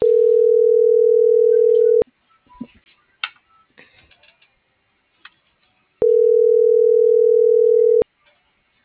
Ringback tone (North America), 440 + 480 Hz.
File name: CENTRAL MABE / out miriam.soledad